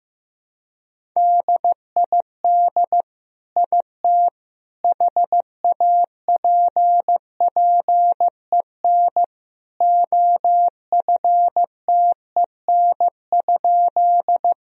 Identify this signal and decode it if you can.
Morse code, heard as a keyed tone, about 15 wpm, 700 Hz. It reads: DID IT HAPPEN OFTEN?
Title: Sentences from Top 1000 Words - Review Part 1 - 15wpm